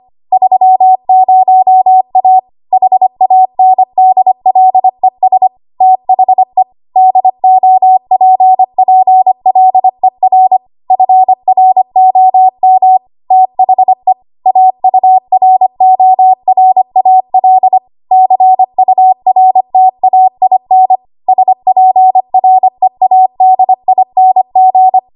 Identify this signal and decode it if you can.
Morse code, heard as a keyed tone, about 25 wpm, 750 Hz. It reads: 30A HANDLES THE DOPPLER FROM THE AURORAL CURTAIN SPREADING
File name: Morse / 25WPM